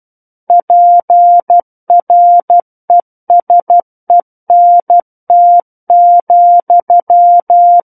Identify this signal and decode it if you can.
Morse code, heard as a keyed tone, about 12 wpm, 700 Hz. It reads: PRESENT,
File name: Book One / 03-Chapter 3 - 12 WPM